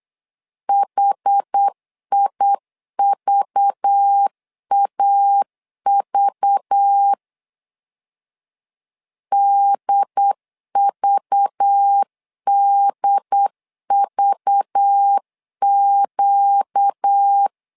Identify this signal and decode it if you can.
Morse code, heard as a keyed tone, about 8 wpm, 800 Hz. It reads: HIVAV DVDVQ
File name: LEZIONI / 5 LBVC / 5 LBV 8wpm 2837